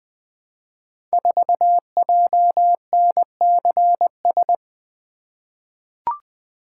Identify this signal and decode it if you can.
Morse code, heard as a keyed tone, about 20 wpm, 700 Hz. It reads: 4JNCS
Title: Code Groups - 5 Letter:Numbers - 20wpm